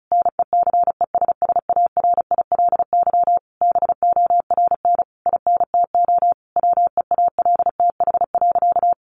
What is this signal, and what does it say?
Morse code, heard as a keyed tone, about 34 wpm, 700 Hz. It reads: NECESSARILY BORN INTO WEALTH.
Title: Sentences from Top 1400 Words 35wpm